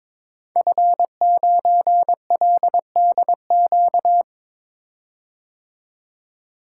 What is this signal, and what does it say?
Morse code, heard as a keyed tone, about 22 wpm, 700 Hz. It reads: F9LDQ
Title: Code Groups - 5 Letter:Numbers - 22wpm